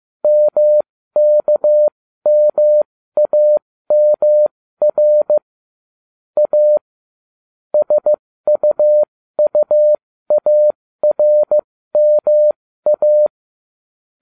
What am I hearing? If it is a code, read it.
Morse code, heard as a keyed tone, about 15 wpm, spaced as for 13 wpm, 600 Hz. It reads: MKMAMR A SUUARMA